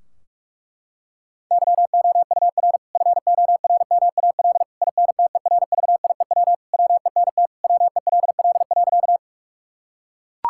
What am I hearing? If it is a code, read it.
Morse code, heard as a keyed tone, about 41 wpm, 700 Hz. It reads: YOUR FORMAL INTERVIEW WENT WELL.
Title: Sentences from Top 1600 Words 45wpm